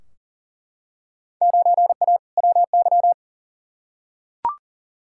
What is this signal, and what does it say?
Morse code, heard as a keyed tone, about 40 wpm, 700 Hz. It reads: 9A WY
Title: Field Day Exchanges Encore - 40wpm